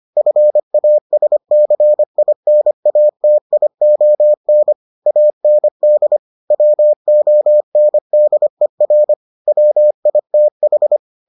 Morse code, heard as a keyed tone, about 25 wpm, 600 Hz. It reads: FASCINATION AND WONDER WITH